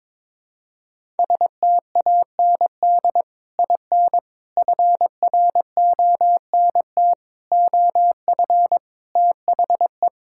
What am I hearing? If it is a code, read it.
Morse code, heard as a keyed tone, about 22 wpm, 700 Hz. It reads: STAND IN FRONT OF THE
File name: practice / 22wpm / Sentences with 3-Word Phrases - 22wpm